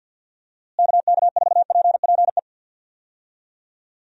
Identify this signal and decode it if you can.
Morse code, heard as a keyed tone, about 48 wpm, 700 Hz. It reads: KK4PPE
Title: USA Call Signs - 6 Characters - Repeated 3x - 50wpm